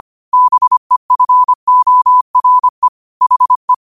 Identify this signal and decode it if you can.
Morse code, heard as a keyed tone, about 25 wpm, 1000 Hz. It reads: BEFORE HE